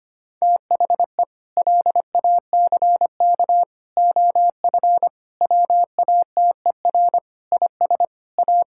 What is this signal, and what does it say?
Morse code, heard as a keyed tone, about 25 wpm, 700 Hz. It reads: THE LACK OF WATER IS A
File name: practice / 25wpm / Sentences with 3-Word Phrases - 25wpm